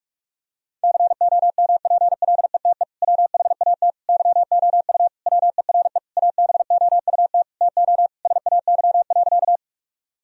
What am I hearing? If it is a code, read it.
Morse code, heard as a keyed tone, about 45 wpm, 700 Hz. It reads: COMPLETE WHAT YOU WERE ABOUT TO SAY.
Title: Sentences from Top 1000 Words - Review Part 2 - 45wpm